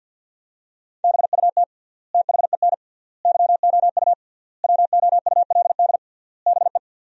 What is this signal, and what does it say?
Morse code, heard as a keyed tone, about 48 wpm, 700 Hz. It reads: BUT THEN YOU WOULD BE
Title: 3x Word-Spacing - Sentences from Top 300 Words - 50wpm